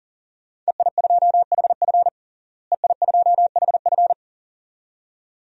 Morse code, heard as a keyed tone, about 40 wpm, 700 Hz. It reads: EI2HF EI2HF
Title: DXCC-100 Call Signs - 5 Characters - Repeated 2x - 40wpm